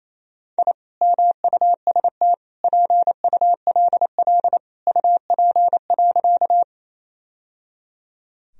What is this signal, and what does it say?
Morse code, heard as a keyed tone, about 28 wpm, 700 Hz. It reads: I MUST PULL UP.
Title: Sentences from Top 1000 Words - Review Part 2 - 28wpm